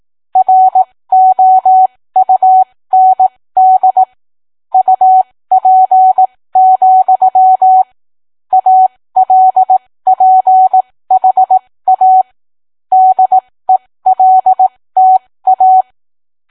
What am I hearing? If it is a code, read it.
Morse code, heard as a keyed tone, about 18 wpm, spaced as for 15 wpm, 750 Hz. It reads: ROUND UP, ALPHA DELTA